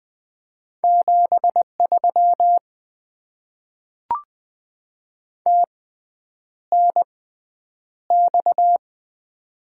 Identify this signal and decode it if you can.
Morse code, heard as a keyed tone, about 20 wpm, spaced as for 9 wpm, 700 Hz. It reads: 73 TNX
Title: Top 90 QSO Elements - 6x Character Spacing - 20wpm